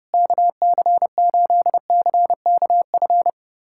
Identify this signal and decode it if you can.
Morse code, heard as a keyed tone, about 30 wpm, 700 Hz. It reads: KC8CKF
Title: USA Callsign DE Callsign - 6 Characters - 30wpm